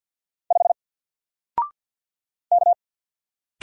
Morse code, heard as a keyed tone, about 48 wpm, 700 Hz. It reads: 5 K
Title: Single Letter-Number Alt 50wpm